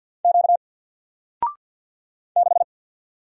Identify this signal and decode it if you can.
Morse code, heard as a keyed tone, about 50 wpm, 700 Hz. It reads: Q 6
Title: Single Letter-Number 50wpm